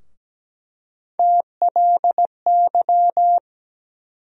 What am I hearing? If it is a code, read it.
Morse code, heard as a keyed tone, about 17 wpm, 700 Hz. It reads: TLY
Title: USA Call Sign Postfixes - 3 Letters 17wpm